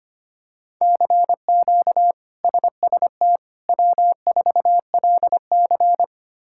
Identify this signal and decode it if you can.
Morse code, heard as a keyed tone, about 25 wpm, 700 Hz. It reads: CQ SST W4LC
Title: K1USN SST QSO Practice - 25wpm